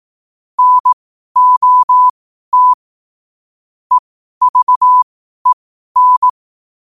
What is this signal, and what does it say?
Morse code, heard as a keyed tone, about 18 wpm, spaced as for 12 wpm, 1000 Hz. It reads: NOT EVEN
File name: qotdfiles / 12wpm